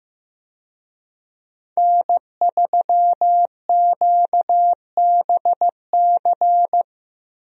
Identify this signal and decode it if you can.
Morse code, heard as a keyed tone, about 15 wpm, 700 Hz. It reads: N3QBC
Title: USA Call Signs - 5 Characters - Repeated 2x - 15wpm